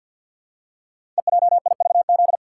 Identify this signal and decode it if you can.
Morse code, heard as a keyed tone, about 48 wpm, 700 Hz. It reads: EJIVZ
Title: Code Groups - 5 Letters - 50wpm